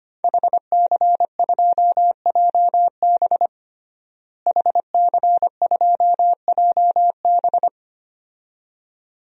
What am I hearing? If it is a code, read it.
Morse code, heard as a keyed tone, about 25 wpm, 700 Hz. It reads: HC2JB HC2JB
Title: DXCC-100 Call Signs - 5 Characters - Repeated 2x - 25wpm